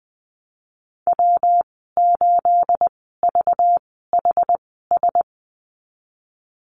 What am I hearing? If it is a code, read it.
Morse code, heard as a keyed tone, about 20 wpm, spaced as for 15 wpm, 700 Hz. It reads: W8VHS
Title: 231 Lesson 37g - Letter x - Call Signs 20wpm